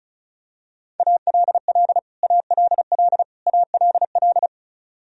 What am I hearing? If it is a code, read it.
Morse code, heard as a keyed tone, about 36 wpm, 700 Hz. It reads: ALL ALL ALL
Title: Top 100 Words Repeated 35wpm